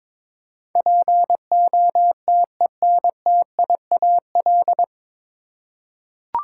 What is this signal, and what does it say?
Morse code, heard as a keyed tone, about 22 wpm, 700 Hz. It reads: POTENTIAL
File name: practice / 22wpm / Nine Letter Words - 22wpm